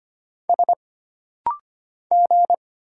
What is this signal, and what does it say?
Morse code, heard as a keyed tone, about 25 wpm, 700 Hz. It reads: S G
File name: Single Character (Letter:Number:Punctuation:Prosign) - Rapid-Fire - 25wpm